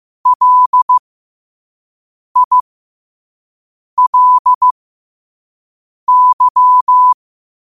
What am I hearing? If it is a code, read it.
Morse code, heard as a keyed tone, about 15 wpm, spaced as for 5 wpm, 1000 Hz. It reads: LILY
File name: qotdfiles / 5wpm